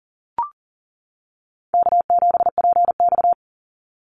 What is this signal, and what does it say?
Morse code, heard as a keyed tone, about 40 wpm, 700 Hz. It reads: K7PX
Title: NA SOTA Chaser Callsigns 40wpm